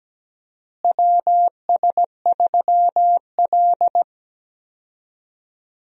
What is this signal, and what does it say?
Morse code, heard as a keyed tone, about 17 wpm, 700 Hz. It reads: WS3L
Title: USA Call Signs - 4 Characters - Repeated 2x - 17wpm